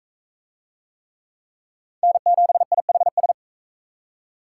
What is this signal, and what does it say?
Morse code, heard as a keyed tone, about 40 wpm, 700 Hz. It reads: N7IHS